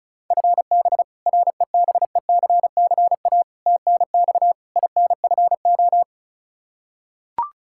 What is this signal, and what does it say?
Morse code, heard as a keyed tone, about 36 wpm, 700 Hz. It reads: FB REBECCA TNX INFO
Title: Practice QSO - Exchange 3 - More Info - 35wpm